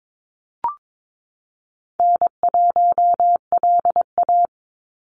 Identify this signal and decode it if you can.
Morse code, heard as a keyed tone, about 22 wpm, 700 Hz. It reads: N1LA
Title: USA Call Signs 22wpm